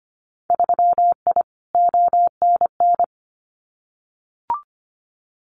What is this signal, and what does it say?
Morse code, heard as a keyed tone, about 25 wpm, 700 Hz. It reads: 3I ONN
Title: Winter Field Day Exchanges Encore 25wpm